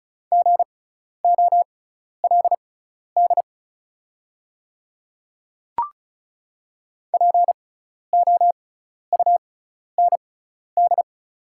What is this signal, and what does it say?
Morse code, heard as a keyed tone, about 36 wpm, spaced as for 12 wpm, 700 Hz. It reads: GOLD POUND
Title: 6x Character Spacing - Top 500 Words - 35wpm